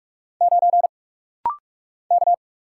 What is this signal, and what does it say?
Morse code, heard as a keyed tone, about 44 wpm, 700 Hz. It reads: 9 K
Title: Single Letter-Number - Rapid-Fire 45wpm